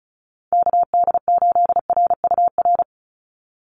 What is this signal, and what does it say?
Morse code, heard as a keyed tone, about 34 wpm, 700 Hz. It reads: KD8RUR
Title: USA Call Signs 35wpm